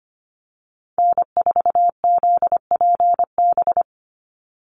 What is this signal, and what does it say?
Morse code, heard as a keyed tone, about 25 wpm, 700 Hz. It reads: N4ZPB
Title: USA Call Signs 25wpm